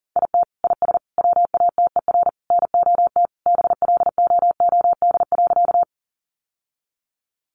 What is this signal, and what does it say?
Morse code, heard as a keyed tone, about 37 wpm, 700 Hz. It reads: IT IS WATER NOT BLOOD.
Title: Sentences from Top 1100 Words 40wpm